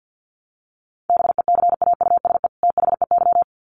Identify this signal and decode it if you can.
Morse code, heard as a keyed tone, about 50 wpm, 700 Hz. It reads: BECAUSE THEY
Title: Sets of 2 Words 50wpm